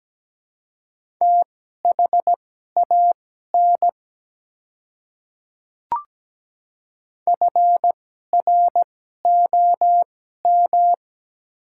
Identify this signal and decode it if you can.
Morse code, heard as a keyed tone, about 17 wpm, spaced as for 12 wpm, 700 Hz. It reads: THAN FROM
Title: 2x Character Spacing - Top 100 Words - 17wpm